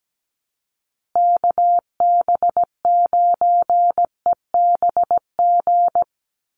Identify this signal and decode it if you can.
Morse code, heard as a keyed tone, about 17 wpm, 700 Hz. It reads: KB9EBG
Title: USA Call Signs Repeated 2x 17wpm